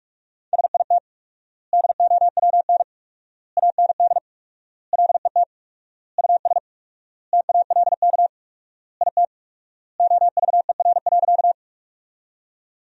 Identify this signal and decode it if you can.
Morse code, heard as a keyed tone, about 41 wpm, 700 Hz. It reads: SIT DOWN AND LET US TALK IT OVER.
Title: 4x Word-Spacing - Sentences from Top 500 Words - 45wpm